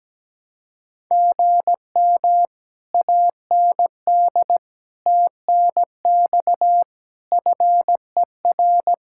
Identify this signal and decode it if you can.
Morse code, heard as a keyed tone, about 17 wpm, 700 Hz. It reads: GM AND TNX FER
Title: Speed-Racing Common QSO Phrases - 17wpm